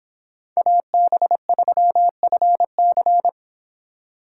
Morse code, heard as a keyed tone, about 25 wpm, 700 Hz. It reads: AB3FC